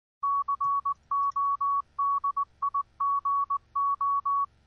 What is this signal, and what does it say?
Morse code, heard as a keyed tone, about 19 wpm, 1120 Hz. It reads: CODIGO